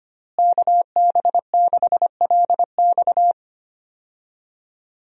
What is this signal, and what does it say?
Morse code, heard as a keyed tone, about 25 wpm, 700 Hz. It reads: KB6LX
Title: USA Call Signs - 5 Characters - Repeated 2x - 25wpm